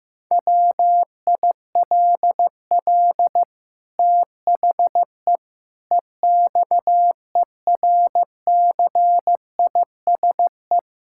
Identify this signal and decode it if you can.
Morse code, heard as a keyed tone, about 15 wpm, 700 Hz. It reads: WILL THE EXERCISE
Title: Sentences from Top 1000 Words - Review Part 6 - 15wpm